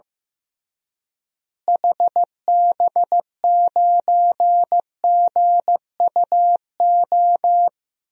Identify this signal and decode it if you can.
Morse code, heard as a keyed tone, about 15 wpm, 700 Hz. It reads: HB9GUO
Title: DXCC-100 Call Signs - 6 Characters - Repeated 3x - 15wpm